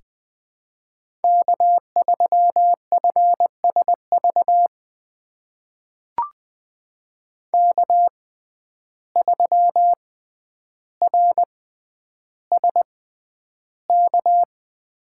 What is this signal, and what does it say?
Morse code, heard as a keyed tone, about 21 wpm, spaced as for 10 wpm, 700 Hz. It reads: K3FSV K3RSK
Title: USA Call Signs - 5 Characters - 6x Character Spacing - 20wpm